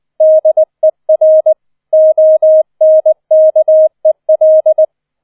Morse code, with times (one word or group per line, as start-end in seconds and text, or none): 0.2-1.5 DER
1.9-4.8 ONKEL